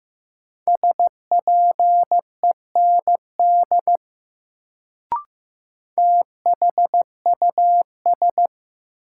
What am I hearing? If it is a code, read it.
Morse code, heard as a keyed tone, about 15 wpm, 700 Hz. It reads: SPEND THUS
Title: Sentences from Top 800 Words - New Word Drill - 15wpm